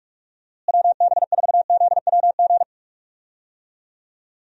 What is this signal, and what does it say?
Morse code, heard as a keyed tone, about 46 wpm, 700 Hz. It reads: WB4ZWG
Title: USA Call Signs - 6 Characters - Repeated 3x - 45wpm